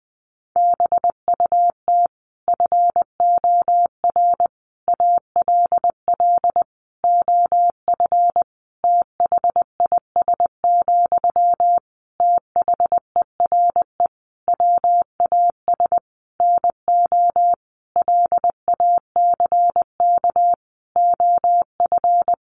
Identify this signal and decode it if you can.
Morse code, heard as a keyed tone, about 20 wpm, 700 Hz. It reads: BUT FOR ALL OF THIS, THERE WAS NO LACK OF
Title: Sentences from Top 1300 Words 20wpm